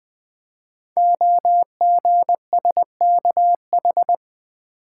Morse code, heard as a keyed tone, about 20 wpm, 700 Hz. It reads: OGSKH
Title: Code Groups - 5 Letters - 20wpm